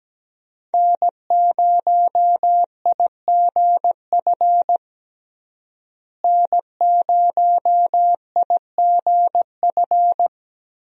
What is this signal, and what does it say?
Morse code, heard as a keyed tone, about 17 wpm, 700 Hz. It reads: N0IGF N0IGF
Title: USA Call Signs - 5 Characters - Repeated 3x - 17wpm